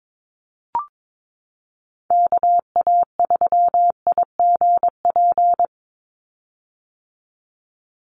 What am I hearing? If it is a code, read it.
Morse code, heard as a keyed tone, about 22 wpm, 700 Hz. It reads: KA3IGP
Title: USA Call Signs 22wpm